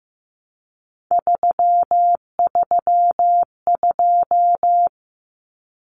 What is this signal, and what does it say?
Morse code, heard as a keyed tone, about 15 wpm, 700 Hz. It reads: 332
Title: Common Cut Numbers - 3-Digit 15wpm